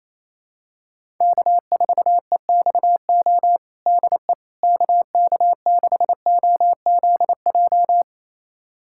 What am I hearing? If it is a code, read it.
Morse code, heard as a keyed tone, about 28 wpm, 700 Hz. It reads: K4EXO DE KK6OZJ
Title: USA Callsign DE Callsign 28wpm